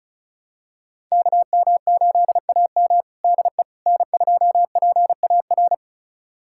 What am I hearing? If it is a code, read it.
Morse code, heard as a keyed tone, about 34 wpm, 700 Hz. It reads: KM8AM DE N2PAR
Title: USA Callsign DE Callsign - 5 Characters - 35wpm